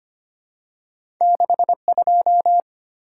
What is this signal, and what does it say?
Morse code, heard as a keyed tone, about 25 wpm, 700 Hz. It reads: 62
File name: Two Letter-Number Combinations - 25wpm